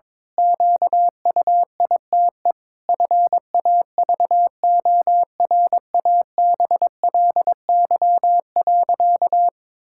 Morse code, heard as a keyed tone, about 22 wpm, 700 Hz. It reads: QUITE FAVORABLY.
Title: Pangram Challenge v2 - 22wpm